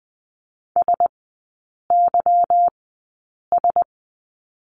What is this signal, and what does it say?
Morse code, heard as a keyed tone, about 20 wpm, spaced as for 8 wpm, 690 Hz. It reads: SYS
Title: 145 Lesson 24e - Letter y - Three Characters 20wpm